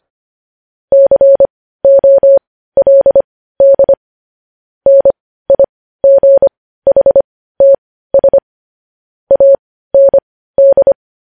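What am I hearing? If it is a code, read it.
Morse code, heard as a keyed tone, about 25 wpm, spaced as for 15 wpm, 550 Hz. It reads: COLD NIGHTS AND
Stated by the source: Morse news headlines sent at 15 wpm.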